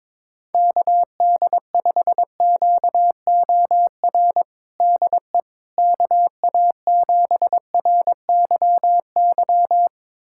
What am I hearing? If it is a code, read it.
Morse code, heard as a keyed tone, about 21 wpm, 700 Hz. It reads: KD5QOR DE KA7RYY
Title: USA Callsign DE Callsign - 6 Characters - 22wpm